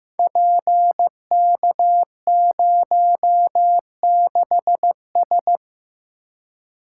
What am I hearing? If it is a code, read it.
Morse code, heard as a keyed tone, about 15 wpm, 700 Hz. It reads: PK06S
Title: Code Groups - 5 Letter:Numbers - 15wpm